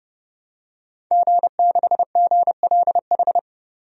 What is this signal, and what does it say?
Morse code, heard as a keyed tone, about 30 wpm, 700 Hz. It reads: G6GLH
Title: DXCC-100 Call Signs - 5 Characters - 30wpm